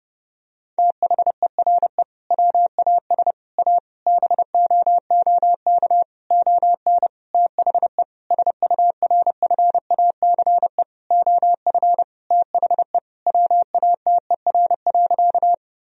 Morse code, heard as a keyed tone, about 30 wpm, 700 Hz. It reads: THERE WAS A BOOK ON THE SURFACE OF THE WATER.
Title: Sentences from Top 1000 Words - Review Part 3 - 30wpm